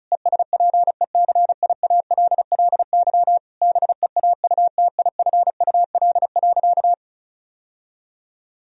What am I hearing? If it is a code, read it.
Morse code, heard as a keyed tone, about 36 wpm, 700 Hz. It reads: ESPECIALLY BEAUTIFUL.
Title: Sentences from Top 1000 Words - Review Part 4 - 35wpm